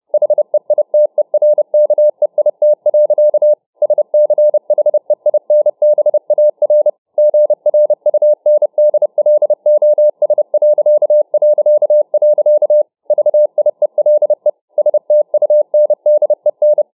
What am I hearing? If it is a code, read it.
Morse code, heard as a keyed tone, about 31 wpm, 600 Hz. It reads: HEITERKEIT. SCHEINBAR GRUNDLOS... VIELE STUNDEN